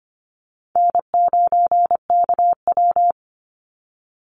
Morse code, heard as a keyed tone, about 25 wpm, 700 Hz. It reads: N9KW
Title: NA SOTA Chaser Callsigns 25wpm